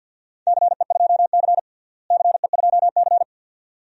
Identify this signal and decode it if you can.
Morse code, heard as a keyed tone, about 51 wpm, 700 Hz. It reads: CE2C CE2C
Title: DXCC-100 Call Signs - 4 Characters - Repeated 2x - 50wpm